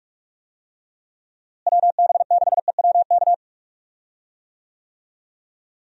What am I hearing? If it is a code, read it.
Morse code, heard as a keyed tone, about 46 wpm, 700 Hz. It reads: WB6EWK
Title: USA Call Signs - 6 Characters - Repeated 3x - 45wpm